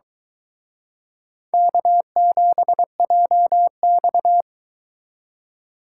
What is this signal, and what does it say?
Morse code, heard as a keyed tone, about 24 wpm, 700 Hz. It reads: K7JX